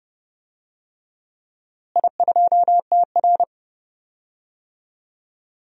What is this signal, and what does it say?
Morse code, heard as a keyed tone, about 32 wpm, 700 Hz. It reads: I2TR
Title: DXCC-100 Call Signs - 4 Characters - Repeated 3x - 30wpm